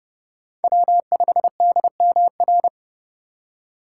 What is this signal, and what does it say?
Morse code, heard as a keyed tone, about 30 wpm, 700 Hz. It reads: W5DMR
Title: USA Call Signs - 5 Characters - 4x Character Spacing - 30wpm